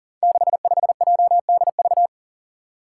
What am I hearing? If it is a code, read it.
Morse code, heard as a keyed tone, about 40 wpm, 700 Hz. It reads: 65JDV
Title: Five Challenging Letter-Number Combinations 40wpm